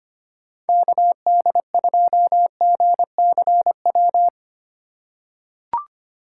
Morse code, heard as a keyed tone, about 25 wpm, 690 Hz. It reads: KD2GCW
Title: USA Call Signs - 6 Characters - 2x Character Spacing - 25wpm